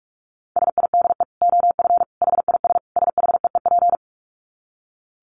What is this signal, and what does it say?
Morse code, heard as a keyed tone, about 44 wpm, 700 Hz. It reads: SIDE OF HIS SHEEP
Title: Dit Challenge 45wpm